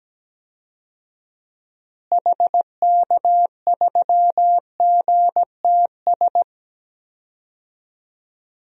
Morse code, heard as a keyed tone, about 17 wpm, 700 Hz. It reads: HK3GTS
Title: DXCC-100 Call Signs - 6 Characters - Repeated 3x - 17wpm